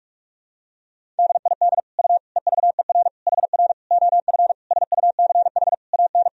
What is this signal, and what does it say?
Morse code, heard as a keyed tone, about 45 wpm, 700 Hz. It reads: DID U EVER HR OF SUCH AN
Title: Sentences from Top 400 Words and Abbreviations 45wpm